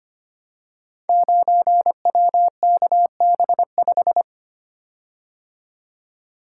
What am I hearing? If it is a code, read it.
Morse code, heard as a keyed tone, about 25 wpm, 700 Hz. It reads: 9WKB5
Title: Code Groups - 5 Letter:Numbers - 25wpm